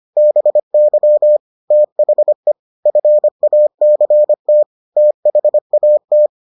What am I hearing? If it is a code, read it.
Morse code, heard as a keyed tone, about 25 wpm, 600 Hz. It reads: BY THE FACT THAT